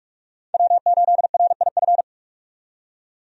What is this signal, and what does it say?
Morse code, heard as a keyed tone, about 46 wpm, 700 Hz. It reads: W8RIF
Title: USA Call Signs - 5 Characters - 45wpm